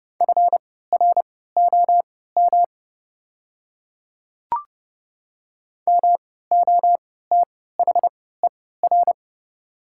Morse code, heard as a keyed tone, about 31 wpm, spaced as for 17 wpm, 690 Hz. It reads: FROM MOTHER
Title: 3x Character Spacing - Top 500 Words - 30wpm